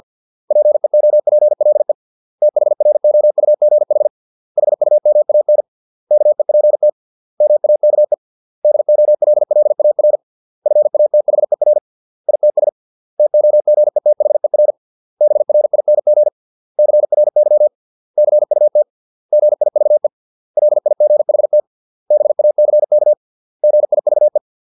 Morse code, heard as a keyed tone, about 50 wpm, 600 Hz. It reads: PEOPLE THROUGH HUMAN KEPT MAKE DOLLAR FATHER ITS TOGETHER BRING CRY CUT GIVE LIGHT BACK GIVE